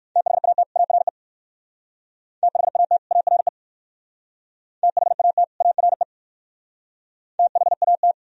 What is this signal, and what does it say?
Morse code, heard as a keyed tone, about 56 wpm, 690 Hz. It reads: THAT ARE THAT ARE THAT ARE THAT
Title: Speed-Racing - Sets of 2 Words 45wpm